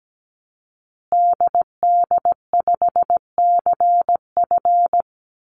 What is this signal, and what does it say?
Morse code, heard as a keyed tone, about 17 wpm, 700 Hz. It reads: DD5CF
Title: Call Signs Repeated 2x - 17wpm